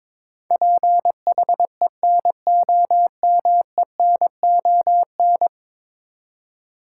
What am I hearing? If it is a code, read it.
Morse code, heard as a keyed tone, about 22 wpm, 700 Hz. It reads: PHENOMENON
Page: morse-code-ninja-practice-22wpm